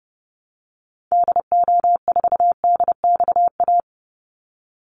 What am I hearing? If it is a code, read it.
Morse code, heard as a keyed tone, about 30 wpm, 700 Hz. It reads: DO4DXA
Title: Call Signs Repeated 3x - 30wpm